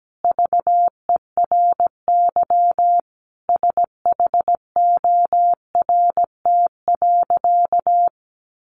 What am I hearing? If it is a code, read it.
Morse code, heard as a keyed tone, about 17 wpm, 700 Hz. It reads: VERY SHORT.
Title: Sentences from Top 1400 Words 17wpm